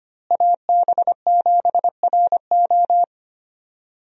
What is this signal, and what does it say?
Morse code, heard as a keyed tone, about 25 wpm, 700 Hz. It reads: AB7RO
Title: USA Call Signs - 5 Characters - Repeated 3x - 25wpm